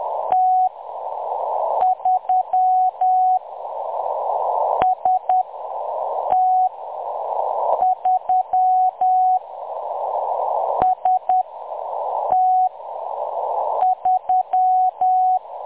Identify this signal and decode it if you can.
Morse code, heard as a keyed tone, about 10 wpm, spaced as for 6 wpm, 740 Hz. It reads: T3 ST3 ST3
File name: M23-ST3- -0858utc-10184khz